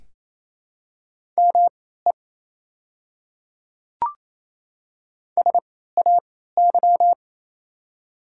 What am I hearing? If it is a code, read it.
Morse code, heard as a keyed tone, about 28 wpm, spaced as for 16 wpm, 700 Hz. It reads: ME SAY
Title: 3x Character Spacing - Top 300 Words - 28wpm